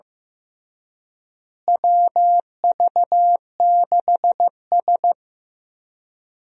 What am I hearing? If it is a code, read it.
Morse code, heard as a keyed tone, about 15 wpm, 700 Hz. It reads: WV6S
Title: USA Call Signs - 4 Characters - Repeated 3x - 15wpm